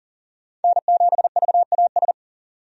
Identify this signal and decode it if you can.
Morse code, heard as a keyed tone, about 40 wpm, 700 Hz. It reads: N7VAS
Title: USA Call Signs - 5 Characters - 40wpm